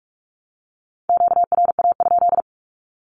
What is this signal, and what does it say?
Morse code, heard as a keyed tone, about 44 wpm, 700 Hz. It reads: QRA?
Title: Q-Codes Repeated 3x 45wpm